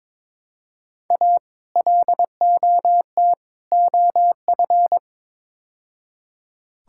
Morse code, heard as a keyed tone, about 22 wpm, 700 Hz. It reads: A LOT OF
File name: practice / 22wpm / Build A Phrase - 5-Word Phrases - 22wpm